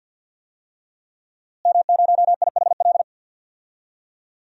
Morse code, heard as a keyed tone, about 50 wpm, 700 Hz. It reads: M0IHL
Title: DXCC-100 Call Signs - 5 Characters - Repeated 3x - 50wpm